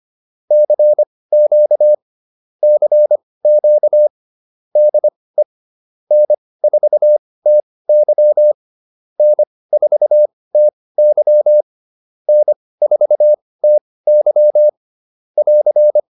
Morse code, heard as a keyed tone, about 25 wpm, spaced as for 18 wpm, 600 Hz. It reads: CQ CQ DE N4TY N4TY N4TY <AR>